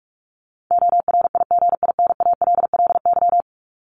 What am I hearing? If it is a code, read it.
Morse code, heard as a keyed tone, about 44 wpm, 700 Hz. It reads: ORIGINALLY
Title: Top 1251-1500 Words 45wpm